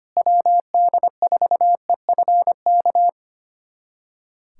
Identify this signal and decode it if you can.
Morse code, heard as a keyed tone, about 25 wpm, 700 Hz. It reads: WD4EFK
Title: USA Callsign DE Callsign 25wpm